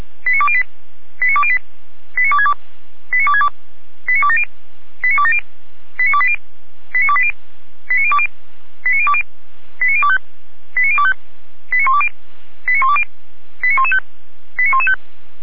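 Five-tone selective call (ZVEI), heard as quick runs of five tones, tones of about 70 ms. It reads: 89208 89208 89262 89262 89270 89270 89280 89280 89020 89020 89025 89025 89120 89120 89115 89115
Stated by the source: ZVEI five-tone call 89340 received on an SDR.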